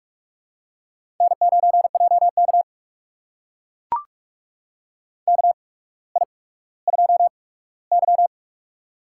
Morse code, heard as a keyed tone, about 44 wpm, spaced as for 19 wpm, 700 Hz. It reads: N9JK KI2Y
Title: USA Call Signs - 4 Characters - 8x Character Spacing - 45wpm